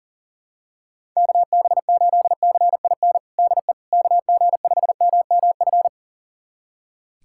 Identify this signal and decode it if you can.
Morse code, heard as a keyed tone, about 37 wpm, 700 Hz. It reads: KB8CIN DE KG5MMF
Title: USA Callsign DE Callsign - 6 Characters - 40wpm